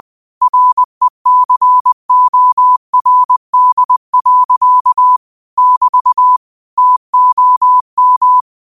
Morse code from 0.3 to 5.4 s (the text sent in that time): RECORD.